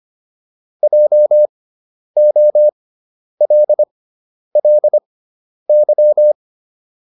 Morse code, heard as a keyed tone, about 25 wpm, spaced as for 10 wpm, 600 Hz. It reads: JOLLY